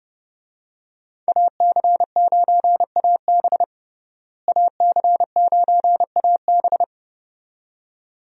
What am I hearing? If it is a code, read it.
Morse code, heard as a keyed tone, about 30 wpm, 700 Hz. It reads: AC9AB AC9AB
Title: USA Call Signs - 5 Characters - Repeated 2x - 30wpm